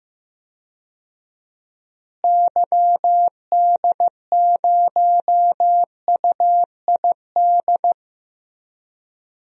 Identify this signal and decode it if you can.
Morse code, heard as a keyed tone, about 15 wpm, 700 Hz. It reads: YD0UID
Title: DXCC-100 Call Signs - 6 Characters - Repeated 3x - 15wpm